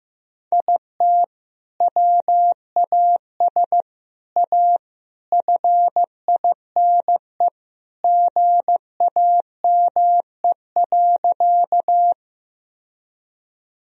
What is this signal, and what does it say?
Morse code, heard as a keyed tone, about 16 wpm, 700 Hz. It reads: IT WAS A FINE GAME.
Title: Sentences from Top 1000 Words - Review Part 3 - 15wpm